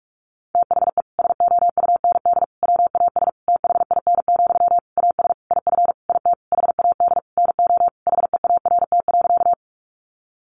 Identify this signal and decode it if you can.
Morse code, heard as a keyed tone, about 44 wpm, 700 Hz. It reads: THE SOUND WAS THIN, AS IF IT HAD NO HEART.
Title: Sentences from Top 1200 Words 45wpm